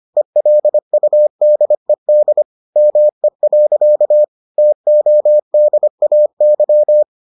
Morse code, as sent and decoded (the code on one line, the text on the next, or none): . .-.. ..- -.. -.. / -- . .-.-.- / - --- -.. .- -.--
ELUDED ME. TODAY